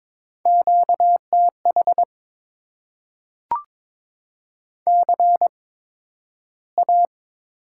Morse code, heard as a keyed tone, about 22 wpm, spaced as for 12 wpm, 700 Hz. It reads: QTH CA
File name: Top 90 QSO Elements - 8x Character Spacing - 22wpm